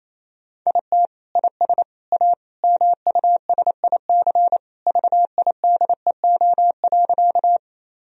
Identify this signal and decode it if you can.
Morse code, heard as a keyed tone, about 28 wpm, 700 Hz. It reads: IT IS A MUSIC VIDEO.
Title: Sentences from Top 1000 Words - Review Part 4 - 28wpm